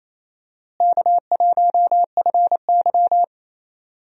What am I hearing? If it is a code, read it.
Morse code, heard as a keyed tone, about 28 wpm, 700 Hz. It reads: K1FY
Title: USA Call Signs - 4 Characters - Repeated 2x - 28wpm